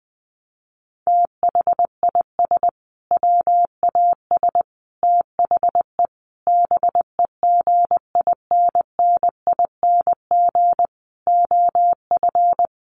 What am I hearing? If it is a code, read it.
Morse code, heard as a keyed tone, about 20 wpm, 700 Hz. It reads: THIS WAS THE BEGINNING OF
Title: Sentences from Top 900 Words 20wpm